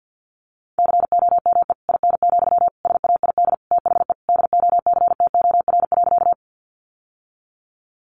Morse code, heard as a keyed tone, about 50 wpm, 700 Hz. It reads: COME IN, SAID THE DOCTOR.
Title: Sentences from Top 1500 Words 50wpm